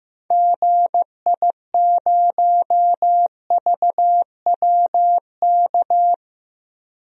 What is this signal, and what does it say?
Morse code, heard as a keyed tone, about 15 wpm, 700 Hz. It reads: GI0VWK
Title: DXCC-100 Call Signs - 6 Characters - 15wpm